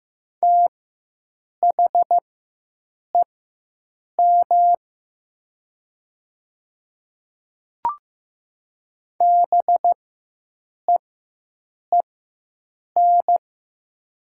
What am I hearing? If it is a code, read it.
Morse code, heard as a keyed tone, about 15 wpm, spaced as for 7 wpm, 700 Hz. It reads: THEM BEEN
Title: 4x Character Spacing - Top 200 Words - 15wpm